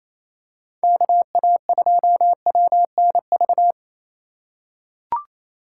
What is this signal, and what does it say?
Morse code, heard as a keyed tone, about 28 wpm, 700 Hz. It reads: KA2WNV
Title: USA Call Signs - 3x Character Spacing - 28wpm